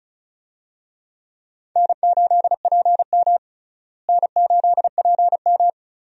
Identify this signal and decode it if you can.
Morse code, heard as a keyed tone, about 36 wpm, 700 Hz. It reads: N8PM N8PM
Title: USA Call Signs - 4 Characters - Repeated 2x - 35wpm